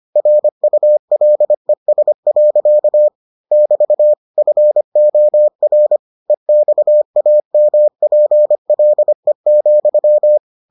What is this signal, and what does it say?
Morse code, heard as a keyed tone, about 25 wpm, 600 Hz. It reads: RULES. <BT> FOR EXAMPLE,